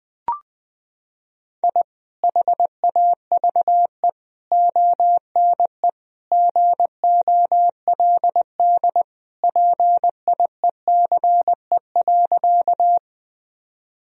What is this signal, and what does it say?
Morse code, heard as a keyed tone, about 20 wpm, 700 Hz. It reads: I HAVE ONE GOLD PIECE.
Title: Sentences from Top 1000 Words - Review Part 3 - 20wpm